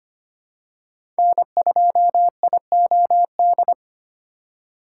Morse code, heard as a keyed tone, about 25 wpm, 700 Hz. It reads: N2IOD
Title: USA Call Signs - 5 Characters - 3x Character Spacing - 25wpm